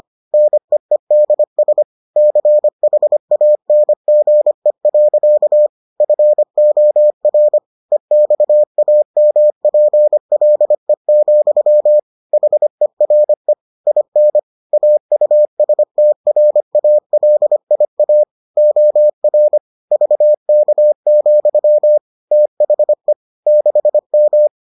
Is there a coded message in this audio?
Yes: Morse code from 0.3 to 24.5 s, heard as a keyed tone, about 25 wpm, 600 Hz. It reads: NEEDS CHANGE. FOR EXAMPLE, HERE IN AUSTRALIA OR VK, THE 6M